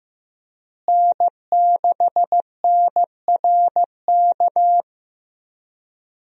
Morse code, heard as a keyed tone, about 15 wpm, 700 Hz. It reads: N6NRK
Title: USA Call Signs - 5 Characters - Repeated 2x - 15wpm